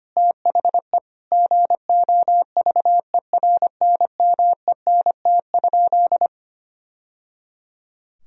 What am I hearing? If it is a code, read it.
Morse code, heard as a keyed tone, about 25 wpm, 700 Hz. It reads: THE GOVERNMENT?
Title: Sentences from Top 1000 Words - Review Part 3 - 25wpm